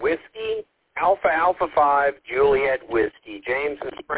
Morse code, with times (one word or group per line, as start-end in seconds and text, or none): none